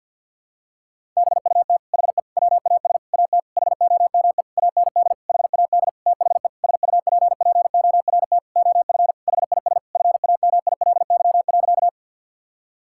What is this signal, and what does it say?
Morse code, heard as a keyed tone, about 50 wpm, 700 Hz. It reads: BUT HE WAS AT HOME AND HAD THE SUPPORT OF HIS FAMILY.
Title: Sentences from Top 1000 Words - Review Part 5 - 50wpm